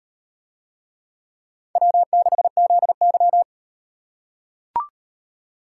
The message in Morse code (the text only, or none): W6ZY